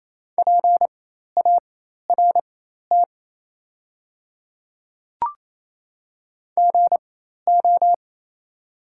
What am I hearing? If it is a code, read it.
Morse code, heard as a keyed tone, about 28 wpm, spaced as for 13 wpm, 700 Hz. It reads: PART GO
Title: 4x Character Spacing - Top 500 Words - 28wpm